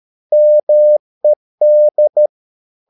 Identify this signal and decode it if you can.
Morse code, heard as a keyed tone, about 13 wpm, 600 Hz. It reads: MED
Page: Ferie 13 wpm